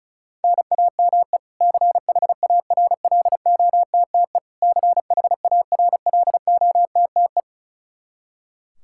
Morse code, heard as a keyed tone, about 36 wpm, 700 Hz. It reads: NAME CHARLOTTE CHARLOTTE
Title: Practice QSO - Exchange 1 - Beginning - 35wpm